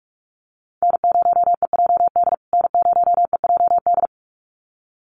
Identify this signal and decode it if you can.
Morse code, heard as a keyed tone, about 44 wpm, 690 Hz. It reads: N0EJD N0EJD
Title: USA Call Signs Repeated 2x 45wpm